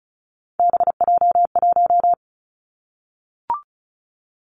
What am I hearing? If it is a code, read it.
Morse code, heard as a keyed tone, about 34 wpm, 700 Hz. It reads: BJ1
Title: Three Challenging Letter-Number Combinations 35wpm